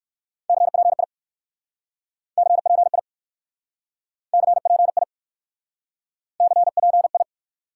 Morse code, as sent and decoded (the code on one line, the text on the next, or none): -.-. .--. .. / -.-. .--. .. / -.-. .--. .. / -.-. .--. ..
CPI CPI CPI CPI